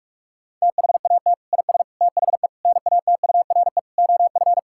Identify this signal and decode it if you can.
Morse code, heard as a keyed tone, about 46 wpm, 700 Hz. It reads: THAT IS THE NATURE OF
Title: Sentences from Top 1000 Words - Review Part 5 - 45wpm